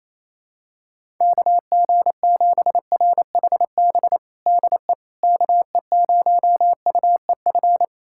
Morse code, heard as a keyed tone, about 28 wpm, 700 Hz. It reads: KG7RHB DE KE0UEF
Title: USA Callsign DE Callsign - 6 Characters - 28wpm